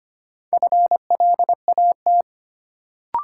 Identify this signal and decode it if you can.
Morse code, heard as a keyed tone, about 25 wpm, 700 Hz. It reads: FLAT
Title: Sentences from Top 1000 Words - New Word Drill - 25wpm